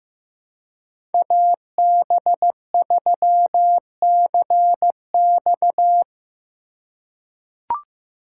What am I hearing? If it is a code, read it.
Morse code, heard as a keyed tone, about 15 wpm, 700 Hz. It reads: AB3CX
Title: USA Call Signs - 3x Character Spacing - 15wpm